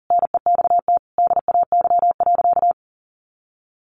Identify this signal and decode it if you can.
Morse code, heard as a keyed tone, about 40 wpm, 700 Hz. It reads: NEXT DAY.
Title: Sentences from Top 1400 Words 40wpm